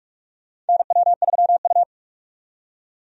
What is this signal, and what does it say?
Morse code, heard as a keyed tone, about 42 wpm, 700 Hz. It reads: NW3U
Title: USA Call Signs - 4 Characters - 45wpm